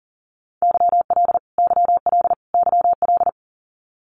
Morse code, heard as a keyed tone, about 40 wpm, 700 Hz. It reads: YL YL YL
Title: Abbreviations Repeated 3x - 40wpm